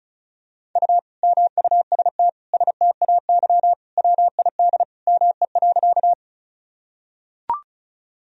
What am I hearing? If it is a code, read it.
Morse code, heard as a keyed tone, about 36 wpm, 700 Hz. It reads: U MUST STAY WID ME.
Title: Sentences from Top 500 Words and Abbreviations 35wpm